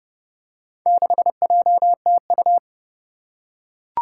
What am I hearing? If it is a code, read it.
Morse code, heard as a keyed tone, about 30 wpm, 700 Hz. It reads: 6JTU
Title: Four Challenging Letter-Number Combinations 30wpm